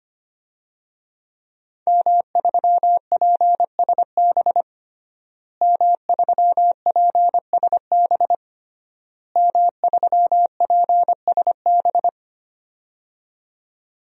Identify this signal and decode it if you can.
Morse code, heard as a keyed tone, about 25 wpm, 700 Hz. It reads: M3PSB M3PSB M3PSB
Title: DXCC-100 Call Signs - 5 Characters - Repeated 3x - 25wpm